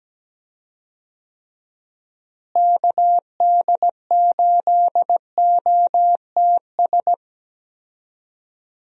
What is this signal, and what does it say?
Morse code, heard as a keyed tone, about 17 wpm, 700 Hz. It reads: KD8OTS
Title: USA Call Signs - 6 Characters - Repeated 3x - 17wpm